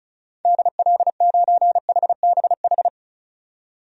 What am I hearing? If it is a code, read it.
Morse code, heard as a keyed tone, about 34 wpm, 700 Hz. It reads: DL9HBH
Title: DXCC-100 Call Signs - 6 Characters - 35wpm